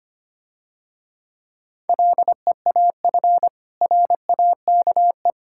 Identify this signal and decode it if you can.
Morse code, heard as a keyed tone, about 29 wpm, 700 Hz. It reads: LEAF RAKE
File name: practice / 25wpm / Difficult Words - Speed-Racing with Hints - 25wpm